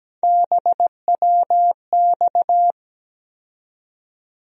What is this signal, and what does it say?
Morse code, heard as a keyed tone, about 17 wpm, 700 Hz. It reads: BWX
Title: USA Call Sign Postfixes - 3 Letters 17wpm